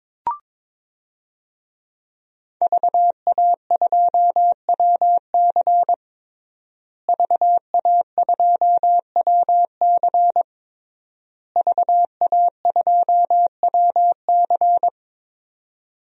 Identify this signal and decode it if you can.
Morse code, heard as a keyed tone, about 23 wpm, 700 Hz. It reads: VA2WC VA2WC VA2WC
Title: DXCC-100 Call Signs - 5 Characters - Repeated 3x - 22wpm